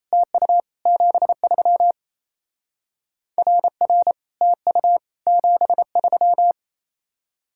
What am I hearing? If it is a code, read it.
Morse code, heard as a keyed tone, about 37 wpm, 700 Hz. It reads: TU 73 RR TU 73